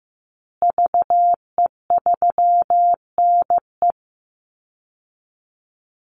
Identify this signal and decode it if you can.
Morse code, heard as a keyed tone, about 15 wpm, 700 Hz. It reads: VE3NE
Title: Call Signs Repeated 2x - 15wpm